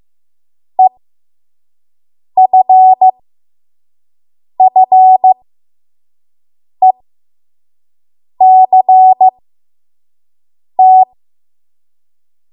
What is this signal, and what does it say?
Morse code, heard as a keyed tone, about 15 wpm, spaced as for 5 wpm, 750 Hz. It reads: EFFECT
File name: Morse / 05WPM